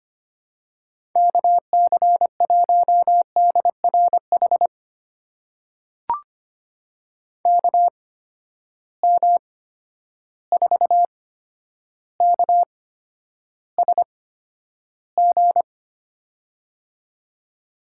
Morse code, heard as a keyed tone, about 25 wpm, spaced as for 11 wpm, 700 Hz. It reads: KC1DRH KM4KSG
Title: USA Call Signs - 6 Characters - 8x Character Spacing - 25wpm